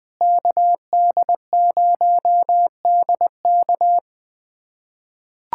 Morse code, heard as a keyed tone, about 20 wpm, 700 Hz. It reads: KD0DK
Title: USA Callsign DE Callsign - 5 Characters - 20wpm